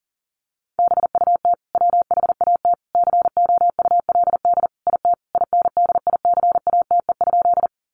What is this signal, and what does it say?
Morse code, heard as a keyed tone, about 40 wpm, 700 Hz. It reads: BUT WHAT COULD IT INDICATE?
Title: Sentences from Top 1300 Words 40wpm